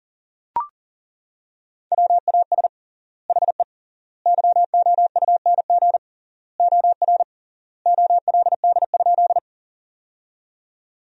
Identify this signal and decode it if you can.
Morse code, heard as a keyed tone, about 40 wpm, 700 Hz. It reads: WAS HE YOUNG OR OLD?
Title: 3x Word-Spacing - Sentences from Top 400 Words - 40wpm